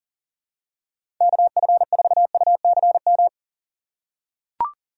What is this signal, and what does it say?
Morse code, heard as a keyed tone, about 41 wpm, 700 Hz. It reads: KF4UCM
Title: USA Call Signs - 3x Character Spacing - 40wpm